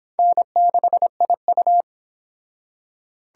Morse code, heard as a keyed tone, about 25 wpm, 700 Hz. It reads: N6IU